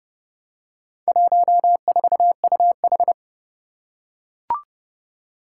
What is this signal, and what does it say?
Morse code, heard as a keyed tone, about 30 wpm, 700 Hz. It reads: 14UH
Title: Four Challenging Letter-Number Combinations 30wpm